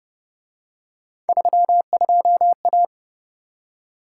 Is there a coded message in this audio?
Yes: Morse code from 1.3 to 2.9 s, heard as a keyed tone, about 29 wpm, 700 Hz. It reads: 32A